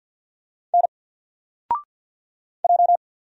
Morse code, heard as a keyed tone, about 48 wpm, 700 Hz. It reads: N J
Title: Single Letter-Number Alt 50wpm